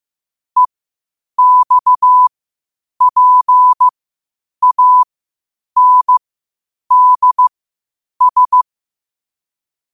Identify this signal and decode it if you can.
Morse code, heard as a keyed tone, about 15 wpm, spaced as for 8 wpm, 1000 Hz. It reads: EXPANDS
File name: qotdfiles / 8wpm